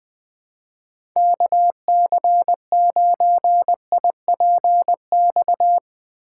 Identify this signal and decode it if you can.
Morse code, heard as a keyed tone, about 20 wpm, 700 Hz. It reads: KC9IPX
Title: USA Call Signs - 6 Characters - 2x Character Spacing - 20wpm